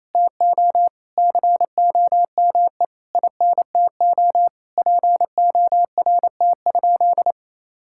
Morse code, heard as a keyed tone, about 28 wpm, 700 Hz. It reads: TO COME INTO PORT?
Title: Sentences from Top 1000 Words - Review Part 5 - 28wpm